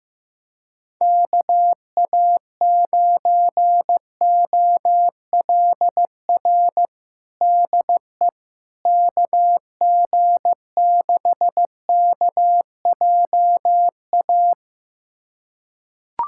Morse code, heard as a keyed tone, about 15 wpm, 700 Hz. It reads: KA9OLR DE KG6KJA
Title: USA Callsign DE Callsign - 6 Characters - 15wpm